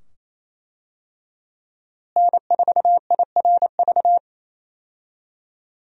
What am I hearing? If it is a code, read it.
Morse code, heard as a keyed tone, about 28 wpm, 700 Hz. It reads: N4IRV
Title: USA Call Signs - 5 Characters - Repeated 2x - 28wpm